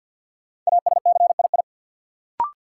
Morse code, heard as a keyed tone, about 47 wpm, 700 Hz. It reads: ASCII
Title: Sets of 5 Words - New Word Drill - 50wpm